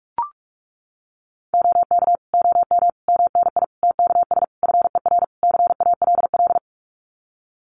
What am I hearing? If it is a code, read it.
Morse code, heard as a keyed tone, about 44 wpm, 700 Hz. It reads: OK OM MNI TKS FER CALL
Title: Common QSO Phrases 45wpm